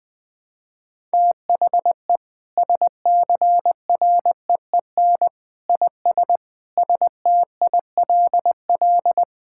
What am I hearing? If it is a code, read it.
Morse code, heard as a keyed tone, about 21 wpm, 700 Hz. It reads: THE SCREEN IS STILL
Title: Sentences from Top 1000 Words - Review Part 6 - 20wpm